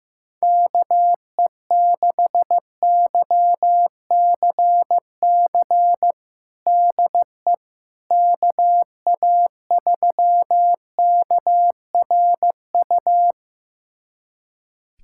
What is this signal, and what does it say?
Morse code, heard as a keyed tone, about 15 wpm, 700 Hz. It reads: KE6YCC DE KA3KRU
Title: USA Callsign DE Callsign 15wpm